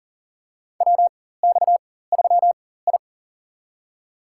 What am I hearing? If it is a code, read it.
Morse code, heard as a keyed tone, about 38 wpm, spaced as for 18 wpm, 700 Hz. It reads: WX3I
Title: USA Call Signs - 4 Characters - 4x Character Spacing - 40wpm